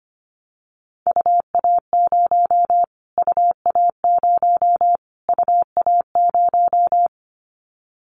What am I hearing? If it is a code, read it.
Morse code, heard as a keyed tone, about 25 wpm, 700 Hz. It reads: UA0 UA0 UA0
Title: Callsign Prefixes for Top 25 DXCC - Repeated 3x - 25wpm